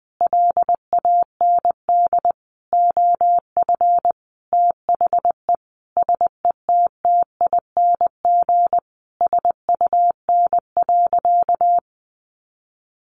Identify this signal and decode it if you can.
Morse code, heard as a keyed tone, about 20 wpm, 700 Hz. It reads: LAND OF THE SETTING SUN.
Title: Sentences from Top 1300 Words 20wpm